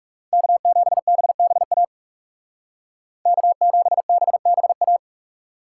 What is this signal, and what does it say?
Morse code, heard as a keyed tone, about 35 wpm, 700 Hz. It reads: K7BBA K7BBA
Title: Speed Racing - USA 1x3 Call Signs 30wpm